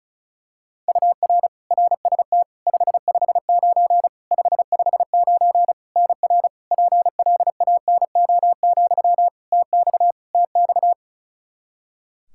Morse code, heard as a keyed tone, about 34 wpm, 700 Hz. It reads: UR RST 559 559 NR PLANO, TX TX
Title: Practice QSO - Exchange 1 - Beginning - 35wpm